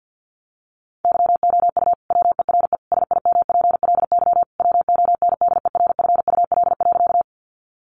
Morse code, heard as a keyed tone, about 50 wpm, 700 Hz. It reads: YOU WERE SIMPLY WONDERFUL.
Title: Sentences from Top 1100 Words 50wpm